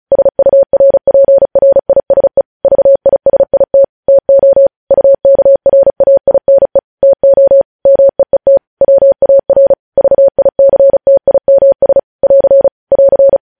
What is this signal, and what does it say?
Morse code, heard as a keyed tone, about 36 wpm, 550 Hz. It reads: SURPRISE VISIT TO UKRAINE TO MEET WAR VICTIMS <AR> <AR>
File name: podcast / News headlines-1x-35x35